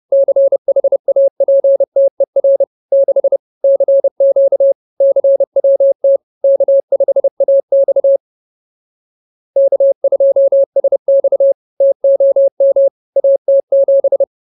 Morse code, heard as a keyed tone, about 30 wpm, 550 Hz. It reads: CHAPTER 6 CQ CWT K5AX K2SX TOM AT7
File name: CWT-Chapter-06